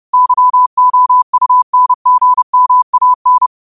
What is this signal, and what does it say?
Morse code, heard as a keyed tone, about 30 wpm, 1000 Hz. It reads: YOUNGMAN
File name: qotdfiles / 30wpm